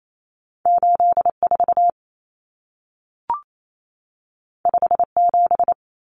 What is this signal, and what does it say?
Morse code, heard as a keyed tone, about 28 wpm, 700 Hz. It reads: 84 57
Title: Common Cut Numbers - 2-Digit 28wpm